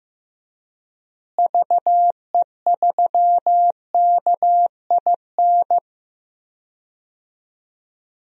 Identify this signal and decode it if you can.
Morse code, heard as a keyed tone, about 16 wpm, 700 Hz. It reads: VE3KIN
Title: DXCC-100 Call Signs - 6 Characters - Repeated 3x - 15wpm